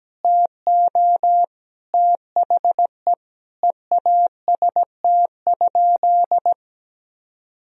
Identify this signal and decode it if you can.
Morse code, heard as a keyed tone, about 17 wpm, 700 Hz. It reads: TO THE EAST?
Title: Sentences from Top 1000 Words - Review Part 3 - 17wpm